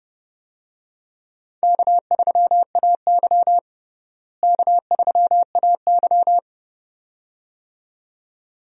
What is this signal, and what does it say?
Morse code, heard as a keyed tone, about 30 wpm, 700 Hz. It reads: K3AY K3AY
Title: USA Call Signs - 4 Characters - Repeated 2x - 30wpm